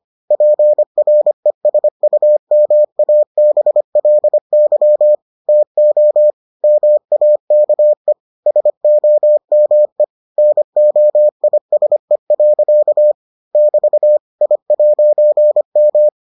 Morse code, heard as a keyed tone, about 25 wpm, 600 Hz. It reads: PRESUMABLY TO MAKE SOME NOISE. <BT> I'M